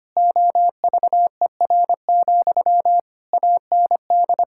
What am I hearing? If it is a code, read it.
Morse code, heard as a keyed tone, about 25 wpm, 700 Hz. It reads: OVER, AND
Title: Sentences from Top 1600 Words 25wpm